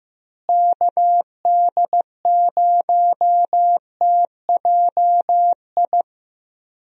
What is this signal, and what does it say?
Morse code, heard as a keyed tone, about 16 wpm, 700 Hz. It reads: KD0TJI
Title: USA Call Signs - 6 Characters - Repeated 3x - 15wpm